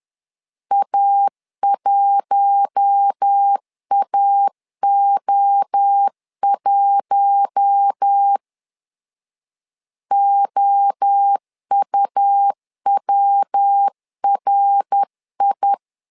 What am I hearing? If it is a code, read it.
Morse code, heard as a keyed tone, about 10 wpm, 800 Hz. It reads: A1AO1 OUWRI
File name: LEZIONI / 4 Q1 10wpm 2448